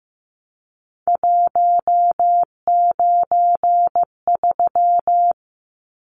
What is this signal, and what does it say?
Morse code, heard as a keyed tone, about 15 wpm, 700 Hz. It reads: 193
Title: Numbers - 3-Digit 15wpm